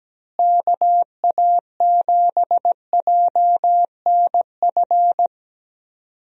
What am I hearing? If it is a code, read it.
Morse code, heard as a keyed tone, about 17 wpm, 700 Hz. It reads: KA7JNF
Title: USA Call Signs - 6 Characters - Repeated 2x - 17wpm